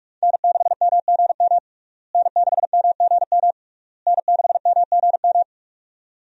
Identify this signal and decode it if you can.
Morse code, heard as a keyed tone, about 46 wpm, 700 Hz. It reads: N6MGM N6MGM N6MGM
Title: USA Call Signs - 5 Characters - Repeated 3x - 45wpm